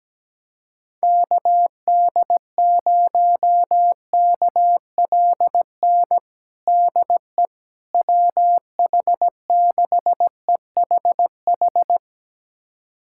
Morse code, heard as a keyed tone, about 17 wpm, 700 Hz. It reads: KD0KLN DE WH6EHH
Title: USA Callsign DE Callsign - 6 Characters - 17wpm